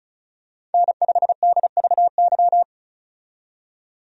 Morse code, heard as a keyed tone, about 34 wpm, 700 Hz. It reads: N5DVY
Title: USA Call Signs - 5 Characters - 35wpm